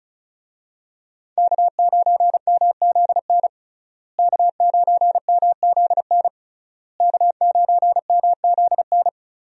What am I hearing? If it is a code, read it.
Morse code, heard as a keyed tone, about 36 wpm, 700 Hz. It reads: K9MZN K9MZN K9MZN
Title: USA Call Signs - 5 Characters - Repeated 3x - 35wpm